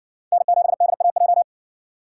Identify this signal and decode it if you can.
Morse code, heard as a keyed tone, about 58 wpm, 700 Hz. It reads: N7DAJ